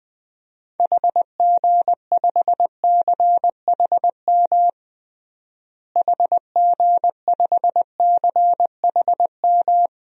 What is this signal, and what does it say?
Morse code, heard as a keyed tone, about 20 wpm, 700 Hz. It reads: HG5CHM HG5CHM
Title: DXCC-100 Call Signs - 6 Characters - Repeated 3x - 20wpm